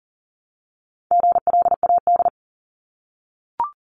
Morse code, heard as a keyed tone, about 39 wpm, 700 Hz. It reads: GLAD
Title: Multi-letter Phonemes 40wpm